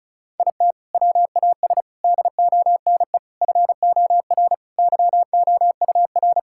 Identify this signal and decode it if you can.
Morse code, heard as a keyed tone, about 36 wpm, 700 Hz. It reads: IT WAS DONE FOR YOUR
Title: Sentences from Top 1600 Words 35wpm